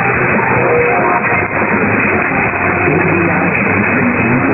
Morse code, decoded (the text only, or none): none